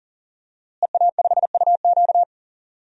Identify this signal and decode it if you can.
Morse code, heard as a keyed tone, about 40 wpm, 700 Hz. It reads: EA5UQ
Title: DXCC-100 Call Signs - 5 Characters - Repeated 2x - 40wpm